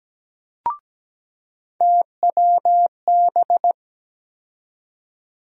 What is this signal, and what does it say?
Morse code, heard as a keyed tone, about 17 wpm, 700 Hz. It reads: TWB
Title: USA Call Sign Postfixes - 3 Letters 17wpm